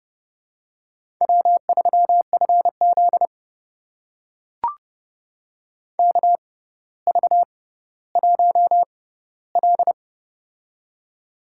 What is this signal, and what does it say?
Morse code, heard as a keyed tone, about 29 wpm, spaced as for 15 wpm, 700 Hz. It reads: W3FZ KV1L
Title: USA Call Signs - 4 Characters - 6x Character Spacing - 30wpm